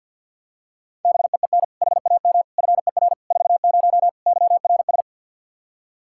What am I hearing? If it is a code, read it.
Morse code, heard as a keyed tone, about 46 wpm, 700 Hz. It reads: BEEN HAM FER 40 YRS
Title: Practice QSO - Exchange 3 - More Info - 50wpm